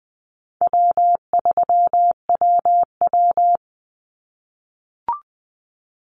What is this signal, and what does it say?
Morse code, heard as a keyed tone, about 20 wpm, 700 Hz. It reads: W3WW
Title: Easy Call Signs - 20wpm